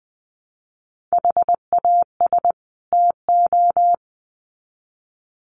Morse code, heard as a keyed tone, about 20 wpm, 700 Hz. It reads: HAS TO
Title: Sets of 2 Words 20wpm